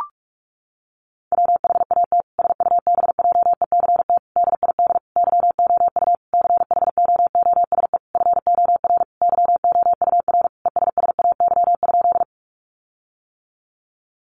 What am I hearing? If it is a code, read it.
Morse code, heard as a keyed tone, about 44 wpm, 700 Hz. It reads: WHAT SUBJECT DID YOU CHOOSE FOR YOUR ESSAY?
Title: Sentences from Top 1500 Words 45wpm